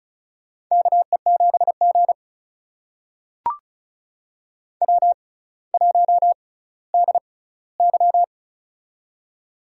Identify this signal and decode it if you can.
Morse code, heard as a keyed tone, about 36 wpm, spaced as for 18 wpm, 700 Hz. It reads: KE7G W1DY
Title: USA Call Signs - 4 Characters - 6x Character Spacing - 35wpm